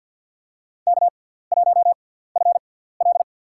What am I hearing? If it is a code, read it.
Morse code, heard as a keyed tone, about 52 wpm, spaced as for 17 wpm, 700 Hz. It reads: K1FL
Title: USA Call Signs - 4 Characters - 6x Character Spacing - 50wpm